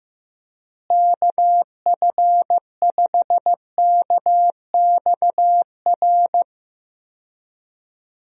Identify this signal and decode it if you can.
Morse code, heard as a keyed tone, about 15 wpm, 700 Hz. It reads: KF5KXR
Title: USA Call Signs - 6 Characters - Repeated 3x - 15wpm